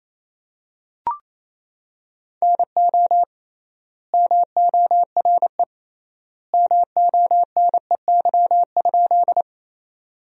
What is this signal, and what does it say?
Morse code, heard as a keyed tone, about 28 wpm, 700 Hz. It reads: NO MORE MONEY?
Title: 3x Word-Spacing - Sentences from Top 400 Words - 28wpm